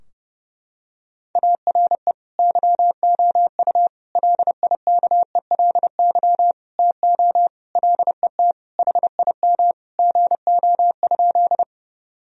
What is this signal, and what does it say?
Morse code, heard as a keyed tone, about 32 wpm, 700 Hz. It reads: ARE YOU LIKELY TO LET HIM GO?
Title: Sentences from Top 1000 Words - Review Part 4 - 30wpm